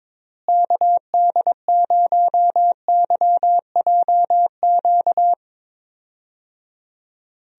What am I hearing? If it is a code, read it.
Morse code, heard as a keyed tone, about 22 wpm, 700 Hz. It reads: KD0YJQ
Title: USA Call Signs - 6 Characters - Repeated 3x - 22wpm